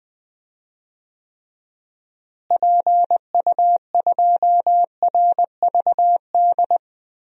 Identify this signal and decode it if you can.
Morse code, heard as a keyed tone, about 20 wpm, 700 Hz. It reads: PU2RVD
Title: DXCC-100 Call Signs - 6 Characters - Repeated 2x - 20wpm